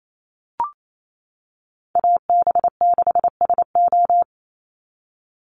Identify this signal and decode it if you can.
Morse code, heard as a keyed tone, about 28 wpm, 700 Hz. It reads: AB6SO
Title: NA SOTA Chaser Callsigns 28wpm